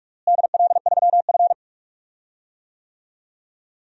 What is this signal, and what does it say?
Morse code, heard as a keyed tone, about 46 wpm, 690 Hz. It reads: DL3F
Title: DXCC-100 Call Signs - 4 Characters - Repeated 3x - 45wpm